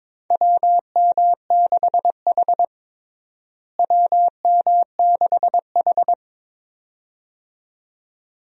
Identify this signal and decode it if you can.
Morse code, heard as a keyed tone, about 22 wpm, 700 Hz. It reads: WM6H WM6H
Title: USA Call Signs - 4 Characters - Repeated 2x - 22wpm